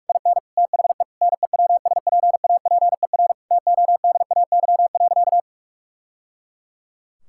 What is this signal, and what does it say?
Morse code, heard as a keyed tone, about 46 wpm, 700 Hz. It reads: IN THE NEWSPAPER TODAY.
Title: Sentences from Top 1000 Words - Review Part 6 - 45wpm